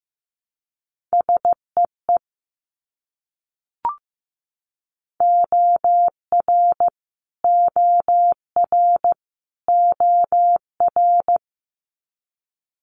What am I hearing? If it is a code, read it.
Morse code, heard as a keyed tone, about 15 wpm, 700 Hz. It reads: SEE OR OR OR
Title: Multi-letter Phonemes Repeated 3x 15wpm